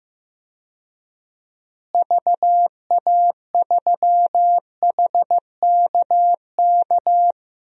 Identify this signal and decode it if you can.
Morse code, heard as a keyed tone, about 16 wpm, 700 Hz. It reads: VA3HKK
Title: DXCC-100 Call Signs - 6 Characters - Repeated 2x - 15wpm